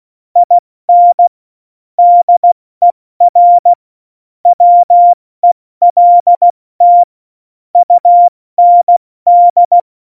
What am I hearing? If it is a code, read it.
Morse code, heard as a keyed tone, about 16 wpm, spaced as for 14 wpm, 700 Hz. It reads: IN DER WELT UND
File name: Morsekurs / Witze / Witz 08. Dezember 2020, 70 F80